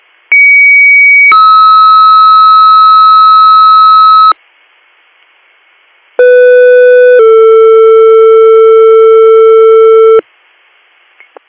Two-tone page, heard as a one-second tone, then a three-second tone, about 2250 then 1290 Hz, about 500 then 450 Hz. It is heard from 0.3 to 10.2 s.